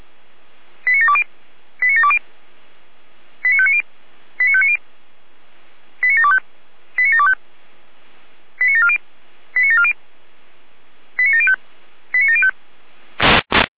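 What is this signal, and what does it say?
Five-tone selective call (ZVEI), heard as quick runs of five tones, tones of about 70 ms. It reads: 89720 89720 89590 89590 89725 89725 89740 89740 89705 89705
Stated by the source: ZVEI five-tone selective call received on an SDR.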